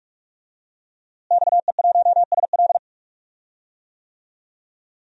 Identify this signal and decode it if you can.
Morse code, heard as a keyed tone, about 41 wpm, 700 Hz. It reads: XE1SL
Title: DXCC-100 Call Signs - 5 Characters - Repeated 3x - 45wpm